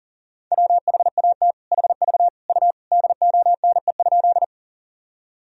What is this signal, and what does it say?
Morse code, heard as a keyed tone, about 40 wpm, 700 Hz. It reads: WHAT HV U DONE?
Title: Sentences from Top 500 Words and Abbreviations 40wpm